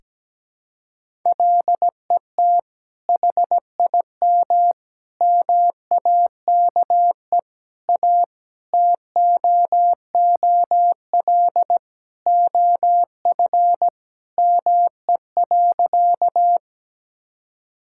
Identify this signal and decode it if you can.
Morse code, heard as a keyed tone, about 17 wpm, 700 Hz. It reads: LET HIM MAKE A TOOL OF ME.
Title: Sentences from Top 1000 Words - Review Part 6 - 17wpm